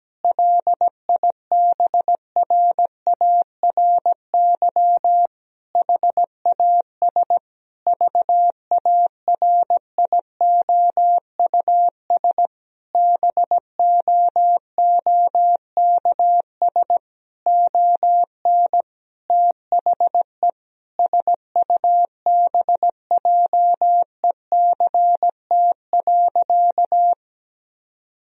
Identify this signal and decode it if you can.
Morse code, heard as a keyed tone, about 17 wpm, 700 Hz. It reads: LIBRARY HAS VARIOUS BOOKS ON THE SUBJECT.
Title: Sentences from Top 1000 Words - Review Part 4 - 17wpm